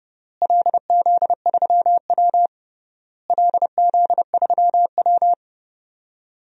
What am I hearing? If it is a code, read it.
Morse code, heard as a keyed tone, about 29 wpm, 700 Hz. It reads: LZ3W LZ3W
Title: DXCC-100 Call Signs - 4 Characters - Repeated 3x - 30wpm